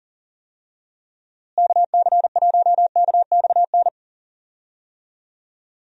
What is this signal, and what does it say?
Morse code, heard as a keyed tone, about 37 wpm, 700 Hz. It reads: KC1KXN
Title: USA Call Signs - 6 Characters - Repeated 2x - 40wpm